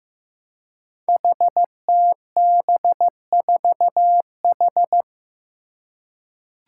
Morse code, heard as a keyed tone, about 15 wpm, 700 Hz. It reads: HTB4H
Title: Five Challenging Letter-Number Combinations 15wpm